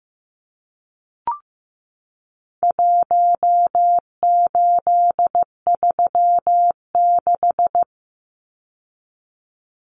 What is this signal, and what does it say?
Morse code, heard as a keyed tone, about 15 wpm, 700 Hz. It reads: 1836
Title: Numbers - Contest Style - 4-Digit 15wpm